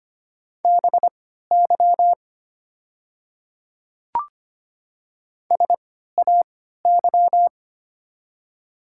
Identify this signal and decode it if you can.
Morse code, heard as a keyed tone, about 25 wpm, spaced as for 14 wpm, 700 Hz. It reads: BY SAY
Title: 3x Character Spacing - Top 300 Words - 25wpm